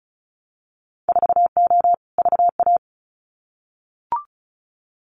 Morse code, heard as a keyed tone, about 36 wpm, 700 Hz. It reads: 4O VA
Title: Winter Field Day Exchanges Encore 35wpm